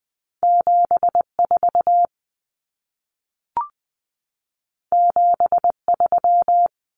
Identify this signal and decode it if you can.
Morse code, heard as a keyed tone, about 20 wpm, 700 Hz. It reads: 74 73
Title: Common Cut Numbers - 2-Digit 20wpm